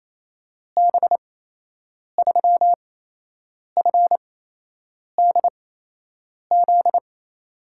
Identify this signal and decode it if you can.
Morse code, heard as a keyed tone, about 27 wpm, spaced as for 8 wpm, 700 Hz. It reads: B3FDZ
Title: USA Call Signs - 6 Characters - 8x Character Spacing - 28wpm